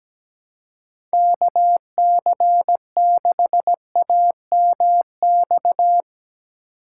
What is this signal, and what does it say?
Morse code, heard as a keyed tone, about 17 wpm, 700 Hz. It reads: KC6AMX
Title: USA Call Signs - 6 Characters - 6x Character Spacing - 17wpm